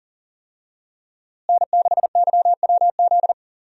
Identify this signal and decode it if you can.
Morse code, heard as a keyed tone, about 40 wpm, 700 Hz. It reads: N6YWZ
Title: USA Call Signs - 5 Characters - Repeated 2x - 40wpm